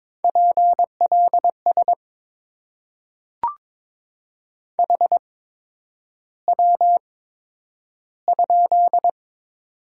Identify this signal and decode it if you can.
Morse code, heard as a keyed tone, about 22 wpm, spaced as for 9 wpm, 700 Hz. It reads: PLS HW?
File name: Top 90 QSO Elements - 8x Character Spacing - 22wpm